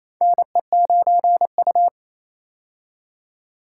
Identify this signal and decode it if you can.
Morse code, heard as a keyed tone, about 27 wpm, 700 Hz. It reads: NE9U
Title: K1USN SST QSO Practice - 28wpm